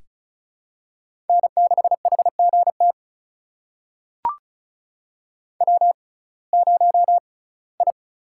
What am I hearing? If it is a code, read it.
Morse code, heard as a keyed tone, about 36 wpm, spaced as for 21 wpm, 700 Hz. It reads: N6HGT W0I
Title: USA Call Signs - 5 Characters - 6x Character Spacing - 35wpm